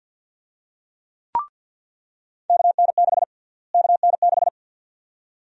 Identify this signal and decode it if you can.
Morse code, heard as a keyed tone, about 46 wpm, 700 Hz. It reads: KN6 KN6
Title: USA Call Sign Prefixes - Repeated 2x - 50wpm